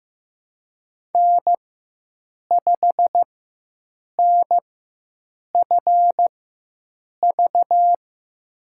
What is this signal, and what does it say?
Morse code, heard as a keyed tone, about 15 wpm, spaced as for 7 wpm, 700 Hz. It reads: N5NFV
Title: USA Call Signs - 5 Characters - 4x Character Spacing - 15wpm